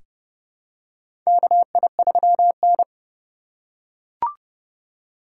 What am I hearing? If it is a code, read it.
Morse code, heard as a keyed tone, about 29 wpm, 700 Hz. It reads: KI3N
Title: USA Call Signs - 3x Character Spacing - 30wpm